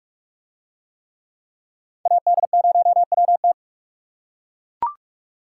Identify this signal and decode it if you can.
Morse code, heard as a keyed tone, about 42 wpm, 700 Hz. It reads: AD0WT